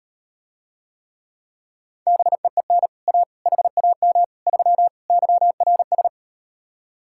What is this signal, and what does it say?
Morse code, heard as a keyed tone, about 38 wpm, 700 Hz. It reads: BEEN A HAM 3 YRS